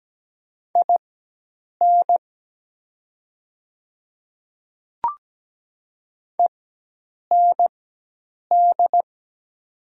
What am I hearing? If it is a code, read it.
Morse code, heard as a keyed tone, about 17 wpm, spaced as for 8 wpm, 700 Hz. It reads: IN END
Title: 4x Character Spacing - Top 500 Words - 17wpm